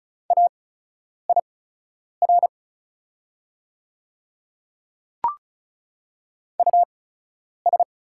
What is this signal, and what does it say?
Morse code, heard as a keyed tone, about 36 wpm, spaced as for 10 wpm, 700 Hz. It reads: AIR US
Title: 8x Character Spacing - Top 200 Words - 35wpm